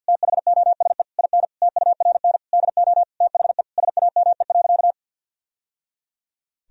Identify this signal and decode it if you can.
Morse code, heard as a keyed tone, about 50 wpm, 700 Hz. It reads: THOSE IN TURN DO THE SAME.
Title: Sentences from Top 1000 Words - Review Part 1 - 50wpm